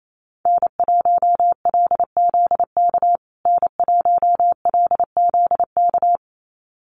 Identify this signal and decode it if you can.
Morse code, heard as a keyed tone, about 28 wpm, 700 Hz. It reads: N1LZK N1LZK
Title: USA Call Signs Repeated 2x 28wpm